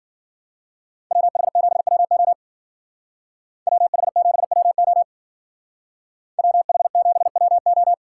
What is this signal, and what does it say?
Morse code, heard as a keyed tone, about 49 wpm, 700 Hz. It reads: WH7WO WH7WO WH7WO
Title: Speed Racing - USA 2x2 Call Signs 40wpm